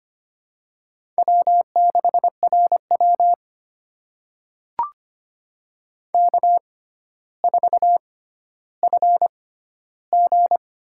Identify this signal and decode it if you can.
Morse code, heard as a keyed tone, about 25 wpm, spaced as for 13 wpm, 700 Hz. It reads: W6RW K4FG
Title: USA Call Signs - 4 Characters - 6x Character Spacing - 25wpm